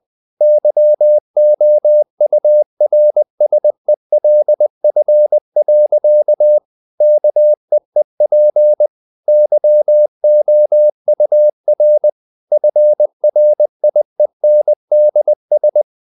Morse code, heard as a keyed tone, about 20 wpm, 600 Hz. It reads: YOURSELF. KEEP YOUR FRIENDS
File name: CW / Practice / EnglishSayings20